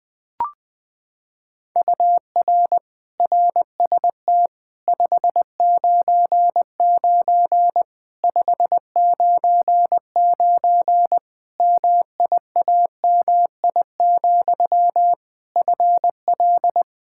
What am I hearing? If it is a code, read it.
Morse code, heard as a keyed tone, about 20 wpm, 700 Hz. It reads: UR RST 599 599 MIAMI, FL
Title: Practice QSO - Exchange 1 - Beginning - 20wpm